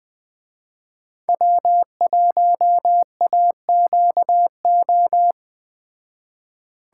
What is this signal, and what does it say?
Morse code, heard as a keyed tone, about 20 wpm, 700 Hz. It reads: W1AQO
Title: Speed Racing - USA 1x3 Call Signs 17wpm